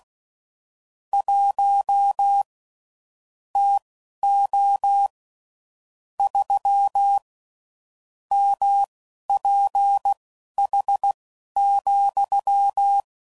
Morse code, heard as a keyed tone, about 16 wpm, spaced as for 11 wpm, 800 Hz. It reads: 1 TO 3 MPH,
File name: CW Code Groups @10WPM / Weather Sets / Weather 5 1